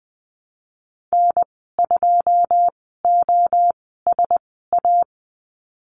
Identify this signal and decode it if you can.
Morse code, heard as a keyed tone, about 20 wpm, spaced as for 15 wpm, 700 Hz. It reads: N2OSA
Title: 088 Lesson 15g - Number 5 - Call Signs 20wpm